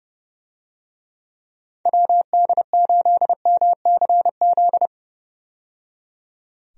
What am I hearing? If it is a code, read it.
Morse code, heard as a keyed tone, about 30 wpm, 700 Hz. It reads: WD8MCZ
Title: USA Call Signs - 6 Characters - 30wpm